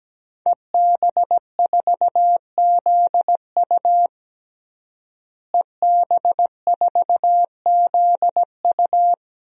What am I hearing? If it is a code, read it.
Morse code, heard as a keyed tone, about 17 wpm, 700 Hz. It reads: EB4ZU EB4ZU
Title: DXCC-100 Call Signs - 5 Characters - Repeated 2x - 17wpm